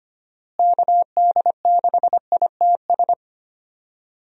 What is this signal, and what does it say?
Morse code, heard as a keyed tone, about 25 wpm, 700 Hz. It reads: KD6ITS
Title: USA Call Signs - 6 Characters - Repeated 2x - 25wpm